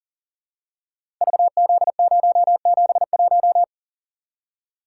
Morse code, heard as a keyed tone, about 38 wpm, 700 Hz. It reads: VZ071
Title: Code Groups - 5 Letter:Numbers - 40wpm